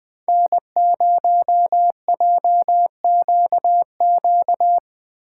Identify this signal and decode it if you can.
Morse code, heard as a keyed tone, about 20 wpm, 700 Hz. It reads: N0JQQ